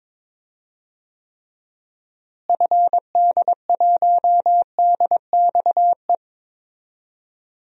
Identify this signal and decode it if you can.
Morse code, heard as a keyed tone, about 21 wpm, 700 Hz. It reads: FD1DXE
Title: DXCC-100 Call Signs - 6 Characters - Repeated 3x - 22wpm